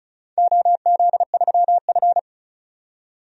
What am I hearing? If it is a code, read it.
Morse code, heard as a keyed tone, about 34 wpm, 700 Hz. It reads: OZ3F
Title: DXCC-100 Call Signs - 4 Characters - 35wpm